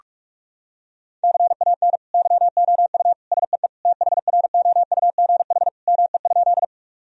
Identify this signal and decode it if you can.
Morse code, heard as a keyed tone, about 46 wpm, 700 Hz. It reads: CAN YOU SEE THROUGH ME?
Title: Sentences from Top 1000 Words - Review Part 1 - 45wpm